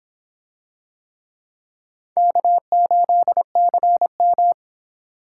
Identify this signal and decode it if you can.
Morse code, heard as a keyed tone, about 26 wpm, 700 Hz. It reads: K8CM